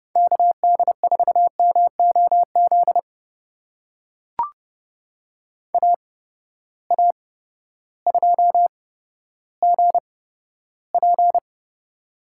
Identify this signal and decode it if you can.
Morse code, heard as a keyed tone, about 32 wpm, spaced as for 14 wpm, 700 Hz. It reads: KD4MOZ AA2GP
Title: USA Call Signs - 8x Character Spacing - 30wpm